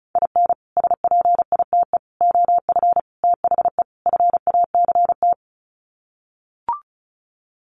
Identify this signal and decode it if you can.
Morse code, heard as a keyed tone, about 34 wpm, 700 Hz. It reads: IN SPITE OF THE FACT
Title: Sets of 5 Words Encore2 35wpm